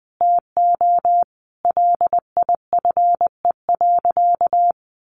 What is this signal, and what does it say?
Morse code, heard as a keyed tone, about 20 wpm, 700 Hz. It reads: TO LIFE.
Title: Sentences from Top 600 Words 20wpm